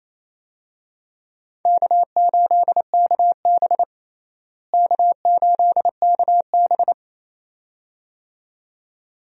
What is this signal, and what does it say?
Morse code, heard as a keyed tone, about 28 wpm, 700 Hz. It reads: K8KB K8KB
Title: USA Call Signs - 4 Characters - Repeated 2x - 28wpm